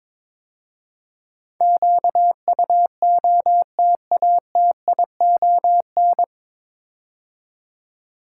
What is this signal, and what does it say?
Morse code, heard as a keyed tone, about 22 wpm, 700 Hz. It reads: QUOTATION
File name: 9 Letter Words - First 3 Letters Given - 22wpm